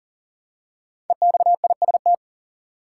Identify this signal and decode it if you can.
Morse code, heard as a keyed tone, about 40 wpm, 700 Hz. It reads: EXIST
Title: Five Letter Words - 40wpm